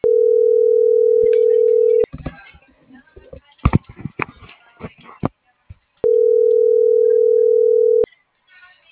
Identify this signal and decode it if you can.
Ringback tone (North America), 440 + 480 Hz.